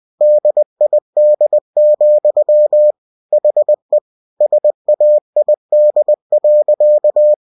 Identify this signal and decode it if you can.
Morse code, heard as a keyed tone, about 20 wpm, 600 Hz. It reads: DID, HE SAID.
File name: cw-books / winnie-the-pooh / winnie-20wpm-page-0033